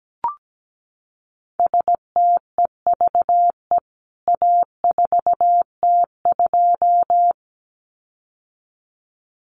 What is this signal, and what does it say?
Morse code, heard as a keyed tone, about 17 wpm, 700 Hz. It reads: STEVE A4T2
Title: CWT Exchange 17wpm